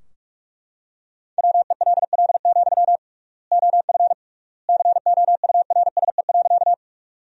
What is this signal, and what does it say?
Morse code, heard as a keyed tone, about 43 wpm, 700 Hz. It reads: WELL, OF COURSE.
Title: 3x Word-Spacing - Sentences from Top 500 Words - 45wpm